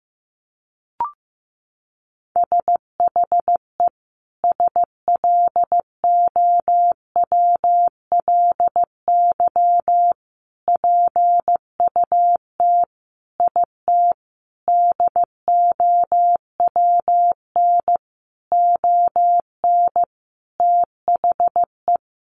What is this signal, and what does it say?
Morse code, heard as a keyed tone, about 15 wpm, 700 Hz. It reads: SHE SLOWLY PUT IT DOWN ON THE
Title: Sentences from Top 1100 Words 15wpm